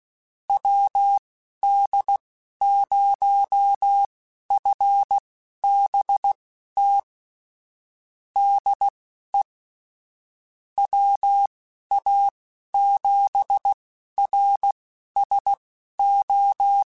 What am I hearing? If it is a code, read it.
Morse code, heard as a keyed tone, about 16 wpm, spaced as for 11 wpm, 800 Hz. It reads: WD0FBT DE WA7RSO
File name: QSOs / QSOs-mp3s / WD0FBT 1@10